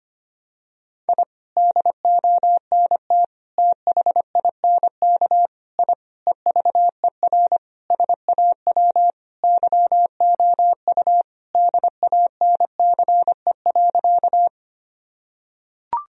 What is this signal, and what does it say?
Morse code, heard as a keyed tone, about 25 wpm, 700 Hz. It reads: I DONT THINK I EVER SAW YOU DANCE.
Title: Sentences from Top 1000 Words - Review Part 4 - 25wpm